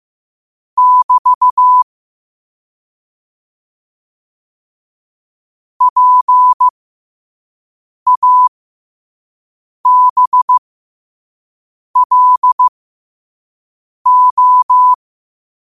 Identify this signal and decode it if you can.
Morse code, heard as a keyed tone, about 15 wpm, spaced as for 5 wpm, 1000 Hz. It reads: <BT> PABLO
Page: New quotes every day in morse code at 5 Words per minute.